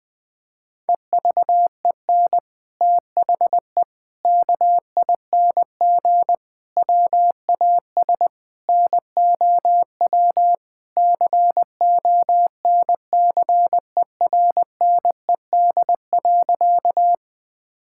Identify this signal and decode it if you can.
Morse code, heard as a keyed tone, about 20 wpm, 700 Hz. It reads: EVEN THE KING WAS NOW CONCERNED.
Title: Sentences from Top 1000 Words - Review Part 6 - 20wpm